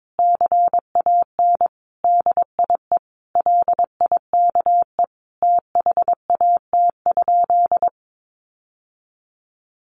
Morse code, heard as a keyed tone, about 22 wpm, 700 Hz. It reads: CAN DIE LIKE THAT?
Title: Sentences from Top 1400 Words 22wpm